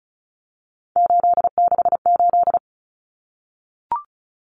Morse code, heard as a keyed tone, about 34 wpm, 700 Hz. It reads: 868
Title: Numbers - 3-Digit 35wpm